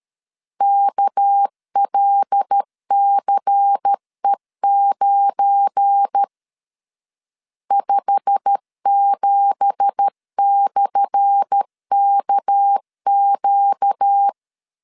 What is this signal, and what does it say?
Morse code, heard as a keyed tone, about 13 wpm, 800 Hz. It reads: KLCE9 57/KQ
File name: LEZIONI / 10 SIGNS / 10A SLASH 12wpm 2314